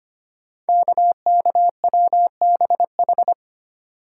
Morse code, heard as a keyed tone, about 25 wpm, 700 Hz. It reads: KKWBH
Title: Code Groups - 5 Letters - 25wpm